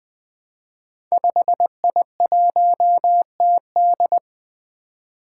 Morse code, heard as a keyed tone, about 20 wpm, 700 Hz. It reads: 5I1TD
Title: Five Challenging Letter-Number Combinations 20wpm